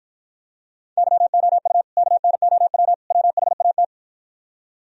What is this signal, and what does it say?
Morse code, heard as a keyed tone, about 52 wpm, 700 Hz. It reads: YOU KNOW WHAT